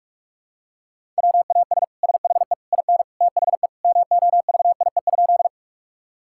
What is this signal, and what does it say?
Morse code, heard as a keyed tone, about 46 wpm, 700 Hz. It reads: WAS SHE IN THE MOVIE?
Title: Sentences from Top 1000 Words - Review Part 5 - 45wpm